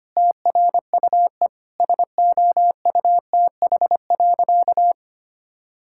Morse code, heard as a keyed tone, about 25 wpm, 700 Hz. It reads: TRUE SOUTH.
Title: Sentences from Top 1000 Words - Review Part 6 - 25wpm